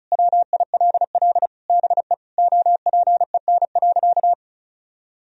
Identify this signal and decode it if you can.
Morse code, heard as a keyed tone, about 36 wpm, 700 Hz. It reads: WILL BE OPEN.
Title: Sentences from Top 1000 Words - Review Part 1 - 35wpm